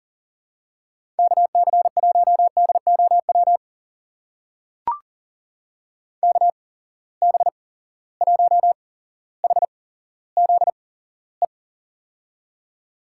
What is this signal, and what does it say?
Morse code, heard as a keyed tone, about 40 wpm, spaced as for 17 wpm, 700 Hz. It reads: KC1DOW KB1HZE
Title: USA Call Signs - 6 Characters - 8x Character Spacing - 40wpm